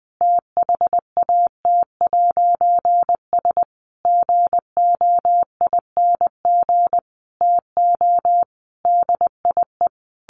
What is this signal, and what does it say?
Morse code, heard as a keyed tone, about 20 wpm, 700 Hz. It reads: THAT'S GOING TO DIE
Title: A Sound Of Thunder 20wpm